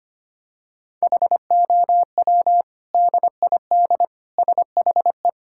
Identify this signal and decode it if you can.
Morse code, heard as a keyed tone, about 25 wpm, 700 Hz. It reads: HOW DID SHE
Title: Sentences from Top 1000 Words - Review Part 5 - 25wpm